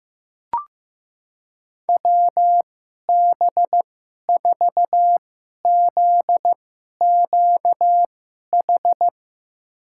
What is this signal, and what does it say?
Morse code, heard as a keyed tone, about 15 wpm, spaced as for 11 wpm, 700 Hz. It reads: WB4ZQH
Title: USA Call Signs - 6 Characters - 2x Character Spacing - 15wpm